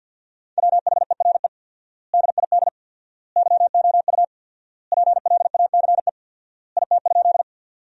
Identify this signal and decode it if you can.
Morse code, heard as a keyed tone, about 50 wpm, 700 Hz. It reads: WHERE DID YOU PLACE IT?
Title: 4x Word-Spacing - Sentences from Top 200 Words - 50wpm